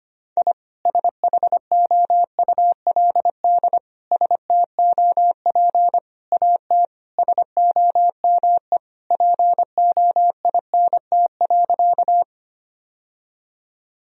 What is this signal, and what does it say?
Morse code, heard as a keyed tone, about 25 wpm, 700 Hz. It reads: I SHOULD STOP AT SOME POINT.
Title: Sentences from Top 1000 Words - Review Part 1 - 25wpm